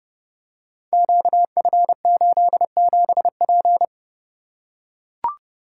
Morse code, heard as a keyed tone, about 30 wpm, 700 Hz. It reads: QF87P
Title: Code Groups - 5 Letter:Numbers - 30wpm